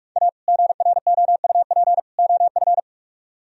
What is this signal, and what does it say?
Morse code, heard as a keyed tone, about 56 wpm, 700 Hz. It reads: A GROUP OF
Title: Speed-Racing - Sets of 3 Words 45wpm